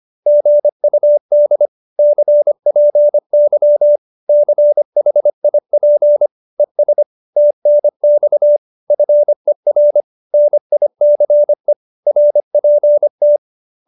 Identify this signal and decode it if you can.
Morse code, heard as a keyed tone, about 25 wpm, 590 Hz. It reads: GUD CPY CHIP ES TNX FER NICE RPT